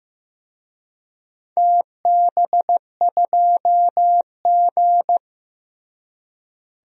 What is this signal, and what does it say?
Morse code, heard as a keyed tone, about 15 wpm, 700 Hz. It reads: TB2G
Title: DXCC-100 Call Signs - 4 Characters - Repeated 3x - 15wpm